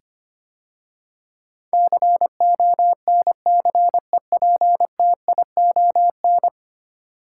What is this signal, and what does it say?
Morse code, heard as a keyed tone, about 25 wpm, 690 Hz. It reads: CONCEPTION
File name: 10 Letter Words - First 3 Letters Given - 25wpm